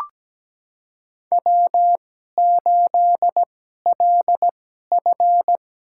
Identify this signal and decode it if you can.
Morse code, heard as a keyed tone, about 17 wpm, spaced as for 12 wpm, 700 Hz. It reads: W8LF
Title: USA Call Signs - 2x Character Spacing - 17wpm